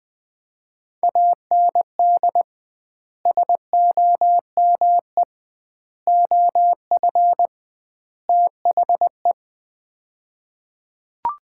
Morse code, heard as a keyed tone, about 21 wpm, 700 Hz. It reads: AND SOME OF THE
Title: 2x Word-Spacing - Sets of 4 Words - 20wpm